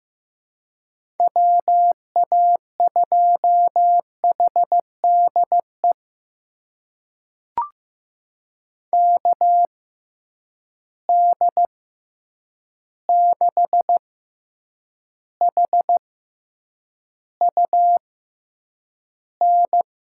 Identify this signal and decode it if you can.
Morse code, heard as a keyed tone, about 15 wpm, spaced as for 8 wpm, 700 Hz. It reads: WA2HDE KD6HUN
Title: USA Call Signs - 6 Characters - 6x Character Spacing - 15wpm